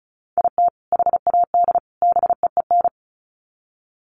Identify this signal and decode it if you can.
Morse code, heard as a keyed tone, about 34 wpm, 700 Hz. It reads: IT HAD BEEN
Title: Sets of 3 Words 35wpm